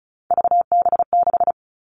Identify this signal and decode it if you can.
Morse code, heard as a keyed tone, about 34 wpm, 700 Hz. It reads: VB6
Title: All Cut Numbers - 3-Digit 35wpm